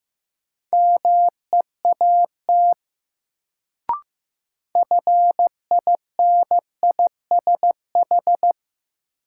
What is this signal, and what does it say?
Morse code, heard as a keyed tone, about 15 wpm, 700 Hz. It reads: MEAT FINISH
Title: Sentences from Top 1000 Words - Review Part 3 - New Word Drill - 15wpm